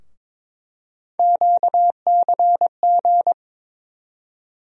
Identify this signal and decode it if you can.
Morse code, heard as a keyed tone, about 22 wpm, 700 Hz. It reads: QCG
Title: USA Call Sign Postfixes - 3 Letters 22wpm